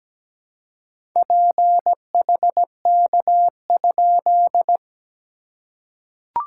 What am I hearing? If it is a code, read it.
Morse code, heard as a keyed tone, about 17 wpm, 690 Hz. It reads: PHK?
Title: SOTA QSO Practice 17wpm